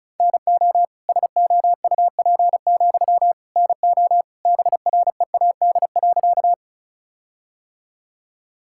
Morse code, heard as a keyed tone, about 36 wpm, 700 Hz. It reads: NO SOUP, NO BREAD.
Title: Sentences from Top 1600 Words 35wpm